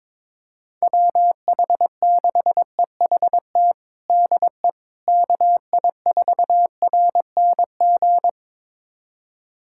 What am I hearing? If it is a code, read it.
Morse code, heard as a keyed tone, about 22 wpm, 700 Hz. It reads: WH6EHT DE KI4RNG
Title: USA Callsign DE Callsign 22wpm